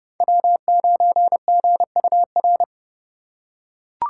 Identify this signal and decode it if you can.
Morse code, heard as a keyed tone, about 28 wpm, 700 Hz. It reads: W9GUR